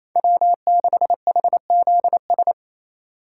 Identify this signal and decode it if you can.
Morse code, heard as a keyed tone, about 28 wpm, 700 Hz. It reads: W6HZS
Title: USA Call Signs - 5 Characters - 2x Character Spacing - 28wpm